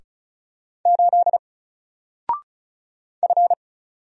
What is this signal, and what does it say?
Morse code, heard as a keyed tone, about 33 wpm, 700 Hz. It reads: 8 F
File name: Single-Letter-Number-35wpm